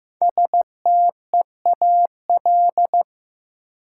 Morse code, heard as a keyed tone, about 15 wpm, 700 Hz. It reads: STEAL
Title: Five Letter Words - 15wpm